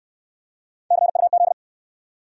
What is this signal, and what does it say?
Morse code, heard as a keyed tone, about 57 wpm, 700 Hz. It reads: CUZ